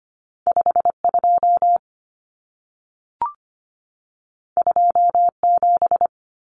Morse code, heard as a keyed tone, about 25 wpm, 700 Hz. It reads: 52 27
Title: Numbers - 2-Digit 25wpm